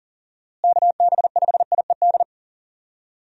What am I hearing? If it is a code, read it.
Morse code, heard as a keyed tone, about 40 wpm, 700 Hz. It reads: KB5IED
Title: USA Call Signs - 6 Characters - Repeated 2x - 40wpm